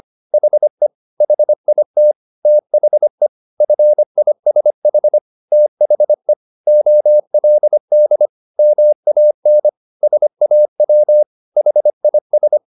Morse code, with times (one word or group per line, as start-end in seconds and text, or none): 0.3-0.9 HE
1.2-2.1 HIT
2.5-3.3 THE
3.6-5.2 FISH
5.5-6.3 THE
6.7-8.3 OLD
8.6-9.7 MAN
10.0-11.2 SAW
11.6-12.6 HIS